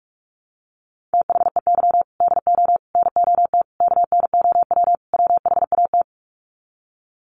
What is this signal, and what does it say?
Morse code, heard as a keyed tone, about 44 wpm, 700 Hz. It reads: THEY DO NOT KNOW WHAT
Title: Sets of 5 Words Encore 45wpm